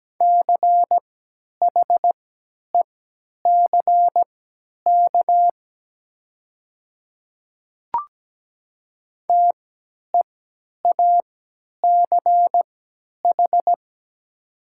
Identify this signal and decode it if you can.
Morse code, heard as a keyed tone, about 17 wpm, spaced as for 10 wpm, 700 Hz. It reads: CHECK TEACH
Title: 3x Character Spacing - Top 500 Words - 17wpm